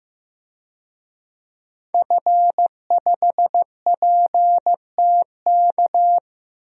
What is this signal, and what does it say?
Morse code, heard as a keyed tone, about 15 wpm, 700 Hz. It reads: F5PTK
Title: DXCC-100 Call Signs - 5 Characters - Repeated 2x - 15wpm